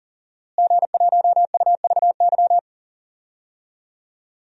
Morse code, heard as a keyed tone, about 37 wpm, 700 Hz. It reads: G1UVY
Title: DXCC-100 Call Signs - 5 Characters - Repeated 2x - 40wpm